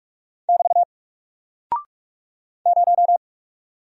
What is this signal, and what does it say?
Morse code, heard as a keyed tone, about 46 wpm, 700 Hz. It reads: <BT> 0
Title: Single Character (Letter:Number:Punctuation:Prosign) Alt 45wpm